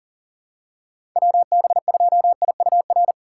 Morse code, heard as a keyed tone, about 38 wpm, 700 Hz. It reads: WB2IUR
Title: USA Call Signs - 6 Characters - 2x Character Spacing - 40wpm